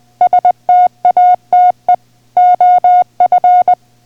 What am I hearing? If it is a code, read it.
Morse code, heard as a keyed tone, about 20 wpm, 720 Hz. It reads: STATE OF